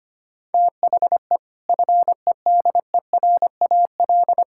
Morse code, heard as a keyed tone, about 25 wpm, 700 Hz. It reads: THE FEDERAL
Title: Sentences from Top 1600 Words 25wpm